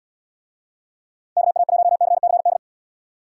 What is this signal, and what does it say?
Morse code, heard as a keyed tone, about 78 wpm, 700 Hz. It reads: KA1CCD